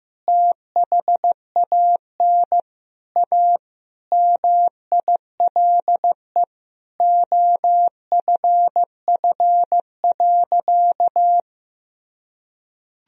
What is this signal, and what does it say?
Morse code, heard as a keyed tone, about 15 wpm, 700 Hz. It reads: THAN A MILE OFF.
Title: Sentences from Top 1000 Words - Review Part 1 - 15wpm